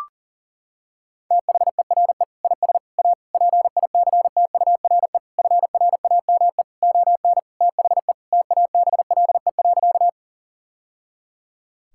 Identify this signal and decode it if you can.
Morse code, heard as a keyed tone, about 40 wpm, 700 Hz. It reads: THERE IS A PICTURE FRAME ON THE TABLE.
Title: Sentences from Top 1000 Words - Review Part 6 - 40wpm